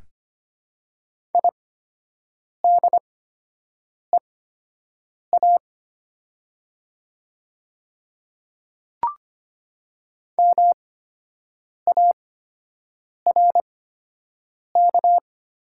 Morse code, heard as a keyed tone, about 25 wpm, spaced as for 7 wpm, 700 Hz. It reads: IDEA MARK
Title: 8x Character Spacing - Top 300 Words - 25wpm